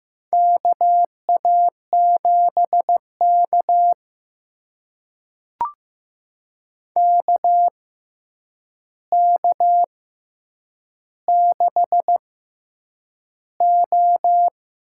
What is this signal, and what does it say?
Morse code, heard as a keyed tone, about 15 wpm, spaced as for 8 wpm, 700 Hz. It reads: KA7K KK6O
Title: USA Call Signs - 4 Characters - 6x Character Spacing - 15wpm